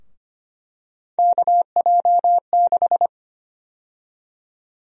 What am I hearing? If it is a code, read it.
Morse code, heard as a keyed tone, about 25 wpm, 700 Hz. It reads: KJ6
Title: USA Call Sign Prefixes - Repeated 2x - 25wpm